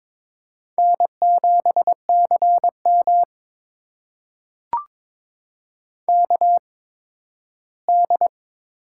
Morse code, heard as a keyed tone, about 22 wpm, spaced as for 13 wpm, 700 Hz. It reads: N7CM KD
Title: USA Call Signs - 4 Characters - 8x Character Spacing - 22wpm